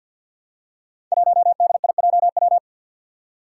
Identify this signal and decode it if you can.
Morse code, heard as a keyed tone, about 53 wpm, 700 Hz. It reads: 1DIJW
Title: Five Challenging Letter-Number Combinations 50wpm